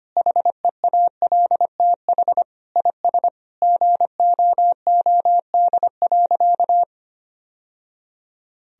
Morse code, heard as a keyed tone, about 25 wpm, 700 Hz. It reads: HEALTH IS GOOD.
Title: Sentences from Top 1000 Words - Review Part 3 - 25wpm